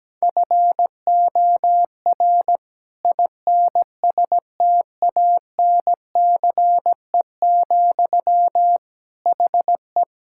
Morse code, heard as a keyed tone, about 17 wpm, 700 Hz. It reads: FOR INSTANCE, HE
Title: Sentences from Top 1000 Words - Review Part 5 - 17wpm